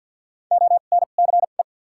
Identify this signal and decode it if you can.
Morse code, heard as a keyed tone, about 53 wpm, 700 Hz. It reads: O N C E
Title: 2x Character Spacing - Top 500 Words - 50wpm